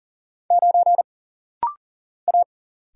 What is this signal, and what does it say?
Morse code, heard as a keyed tone, about 40 wpm, 700 Hz. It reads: 9 A
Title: Single Letter-Number - Rapid-Fire 40wpm